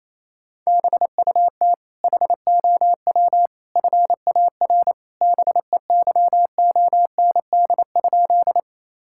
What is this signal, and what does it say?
Morse code, heard as a keyed tone, about 28 wpm, 700 Hz. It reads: BUT HOW FAR BEYOND?
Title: Sentences from Top 1000 Words - Review Part 6 - 28wpm